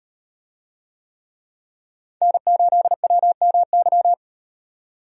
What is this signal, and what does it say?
Morse code, heard as a keyed tone, about 37 wpm, 700 Hz. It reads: N8WMY